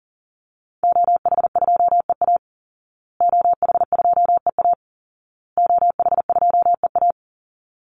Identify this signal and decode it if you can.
Morse code, heard as a keyed tone, about 40 wpm, 700 Hz. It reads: OH2EA OH2EA OH2EA
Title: Call Signs Repeated 2x - 40wpm